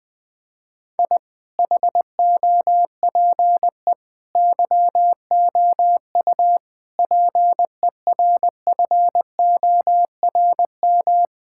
Morse code, heard as a keyed tone, about 20 wpm, 700 Hz. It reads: I HOPE YOU PERFORM
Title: Sentences from Top 1000 Words - Review Part 6 - 20wpm